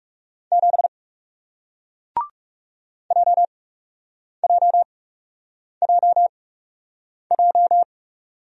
Morse code, heard as a keyed tone, about 40 wpm, spaced as for 8 wpm, 700 Hz. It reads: 7 JJJJ